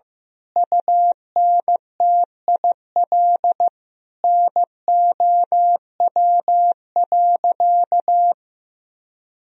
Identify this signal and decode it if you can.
Morse code, heard as a keyed tone, about 15 wpm, 700 Hz. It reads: UNTIL NOW.
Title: Sentences from Top 1000 Words - Review Part 3 - 15wpm